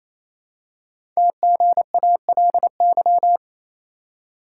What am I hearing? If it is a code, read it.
Morse code, heard as a keyed tone, about 27 wpm, 700 Hz. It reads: TGALY
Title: Code Groups - 5 Letters - 28wpm